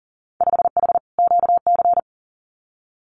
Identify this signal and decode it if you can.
Morse code, heard as a keyed tone, about 40 wpm, 700 Hz. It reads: 5H QC
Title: Winter Field Day Exchanges Encore 40wpm